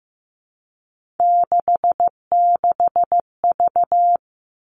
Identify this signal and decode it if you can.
Morse code, heard as a keyed tone, about 15 wpm, 700 Hz. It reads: 66V
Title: Three Challenging Letter-Number Combinations 15wpm